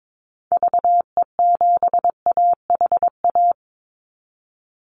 Morse code, heard as a keyed tone, about 22 wpm, 700 Hz. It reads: VE7AHA
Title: Call Signs Repeated 2x - 22wpm